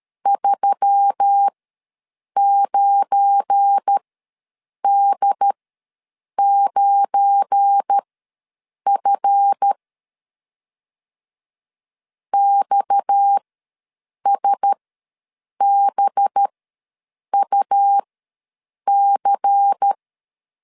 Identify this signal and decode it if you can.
Morse code, heard as a keyed tone, about 13 wpm, spaced as for 7 wpm, 800 Hz. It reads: 39D9F XSBUC
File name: LEZIONI / 7 GF9Z / 7 GF9 ws 12wpm 3344